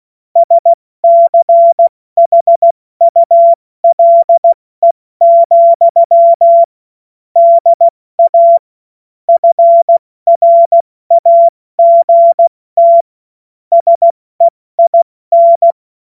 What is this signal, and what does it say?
Morse code, heard as a keyed tone, about 16 wpm, 680 Hz. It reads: SCHULE, DA FRAGT SEIN
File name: Morsekurs / Witze / Witz 07. September 2021, 70 F80